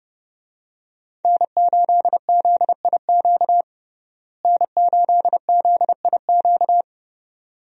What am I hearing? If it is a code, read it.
Morse code, heard as a keyed tone, about 30 wpm, 700 Hz. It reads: N8ZIQ N8ZIQ
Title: USA Call Signs - 5 Characters - Repeated 2x - 30wpm